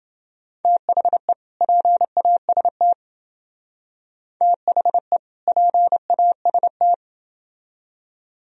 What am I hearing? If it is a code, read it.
Morse code, heard as a keyed tone, about 29 wpm, 700 Hz. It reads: THE PAST THE PAST